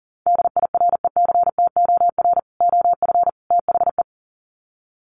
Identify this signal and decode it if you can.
Morse code, heard as a keyed tone, about 37 wpm, 700 Hz. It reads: DIRECTOR OF THE
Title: Sets of 3 Words 40wpm